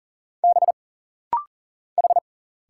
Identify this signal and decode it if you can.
Morse code, heard as a keyed tone, about 39 wpm, 700 Hz. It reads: B H
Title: Single Letter-Number Alt - Rapid-Fire 40wpm